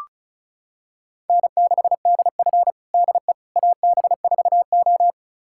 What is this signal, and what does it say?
Morse code, heard as a keyed tone, about 36 wpm, 700 Hz. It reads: N6DF DE AB4O
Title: USA Callsign DE Callsign - 4 Characters - 35wpm